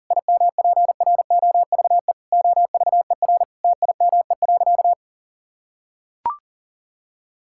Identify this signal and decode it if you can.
Morse code, heard as a keyed tone, about 40 wpm, 700 Hz. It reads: IMPROVE OVER TIME.
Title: Sentences from Top 1000 Words - Review Part 5 - 40wpm